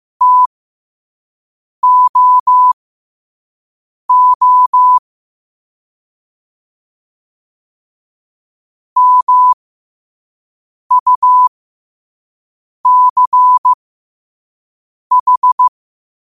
Morse code, heard as a keyed tone, about 15 wpm, spaced as for 5 wpm, 1000 Hz. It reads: TOO MUCH
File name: qotdfiles / 5wpm